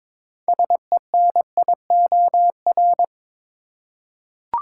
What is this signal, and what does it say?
Morse code, heard as a keyed tone, about 22 wpm, 700 Hz. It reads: SENIOR
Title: Six Letter Words - 22wpm